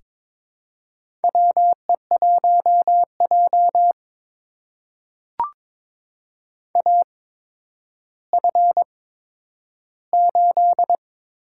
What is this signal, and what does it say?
Morse code, heard as a keyed tone, about 22 wpm, spaced as for 11 wpm, 700 Hz. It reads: WE1J AF8
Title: USA Call Signs - 4 Characters - 8x Character Spacing - 22wpm